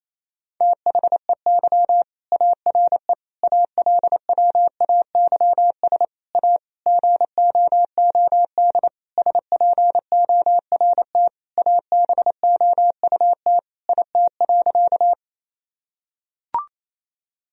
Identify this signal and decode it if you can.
Morse code, heard as a keyed tone, about 28 wpm, 700 Hz. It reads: THEY ARE ALWAYS A GOOD SPORT ABOUT IT.
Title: Sentences from Top 1000 Words - Review Part 4 - 28wpm